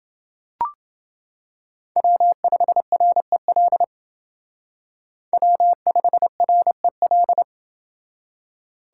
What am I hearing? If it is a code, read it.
Morse code, heard as a keyed tone, about 28 wpm, 700 Hz. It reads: W5REL W5REL